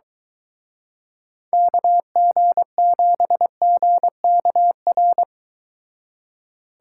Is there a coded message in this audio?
Yes: Morse code from 1.5 to 5.2 s, heard as a keyed tone, about 23 wpm, 700 Hz. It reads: KG7GKR